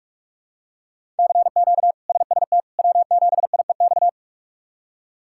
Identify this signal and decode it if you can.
Morse code, heard as a keyed tone, about 46 wpm, 700 Hz. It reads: CQ SST W7IEX
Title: K1USN SST QSO Practice - 45wpm